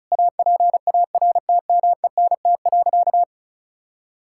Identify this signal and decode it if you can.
Morse code, heard as a keyed tone, about 36 wpm, 700 Hz. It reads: APARTMENT.
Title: Sentences from Top 1000 Words - Review Part 6 - 35wpm